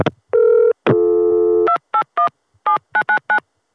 DTMF, heard as telephone touch tones, tones of about 90 ms. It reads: 2817999